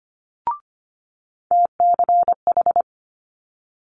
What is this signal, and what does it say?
Morse code, heard as a keyed tone, about 25 wpm, 700 Hz. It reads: TCH
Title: Multi-letter Phonemes Quick Fire 25wpm